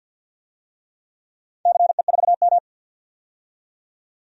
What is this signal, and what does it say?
Morse code, heard as a keyed tone, about 50 wpm, 700 Hz. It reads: CE4M
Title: DXCC-100 Call Signs - 4 Characters - 50wpm